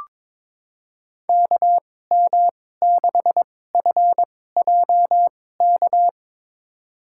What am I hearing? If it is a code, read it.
Morse code, heard as a keyed tone, about 22 wpm, spaced as for 16 wpm, 700 Hz. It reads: KM6FJK
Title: USA Call Signs - 6 Characters - 2x Character Spacing - 22wpm